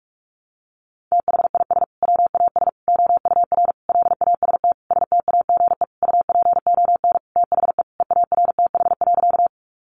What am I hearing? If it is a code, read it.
Morse code, heard as a keyed tone, about 44 wpm, 700 Hz. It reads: THIS WAS OUR LAST STAGE UPON THE EARTH.
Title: Sentences from Top 1200 Words 45wpm